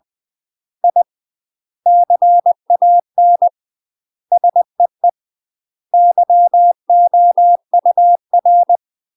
Morse code, heard as a keyed tone, about 20 wpm, 700 Hz. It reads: I CAN SEE YOUR
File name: UK MASTER QSOs TRAINING / 20WPM / UK MASTER QSOs 0006